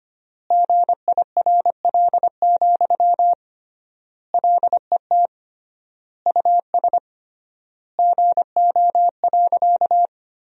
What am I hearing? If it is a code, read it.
Morse code, heard as a keyed tone, about 25 wpm, 700 Hz. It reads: GIRL, LET US GO.
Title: 3x Word-Spacing - Sentences from Top 400 Words - 25wpm